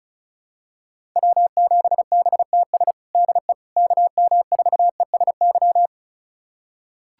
Morse code, heard as a keyed tone, about 36 wpm, 700 Hz. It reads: W7BTS DE KM4ESY
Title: USA Callsign DE Callsign 35wpm